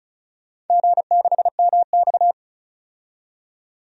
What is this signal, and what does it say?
Morse code, heard as a keyed tone, about 34 wpm, 700 Hz. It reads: G6MX
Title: DXCC-100 Call Signs - 4 Characters - 35wpm